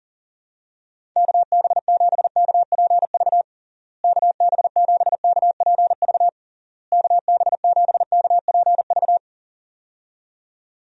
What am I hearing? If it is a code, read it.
Morse code, heard as a keyed tone, about 40 wpm, 700 Hz. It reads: KB7KPV KB7KPV KB7KPV
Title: USA Call Signs - 6 Characters - Repeated 3x - 40wpm